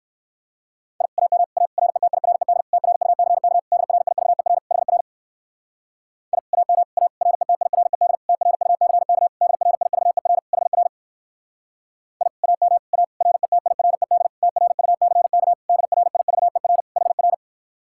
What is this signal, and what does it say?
Morse code, heard as a keyed tone, about 58 wpm, 700 Hz. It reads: I AM A RETIRED TRUCK DRIVER HR I AM A RETIRED TRUCK DRIVER HR I AM A RETIRED TRUCK DRIVER HR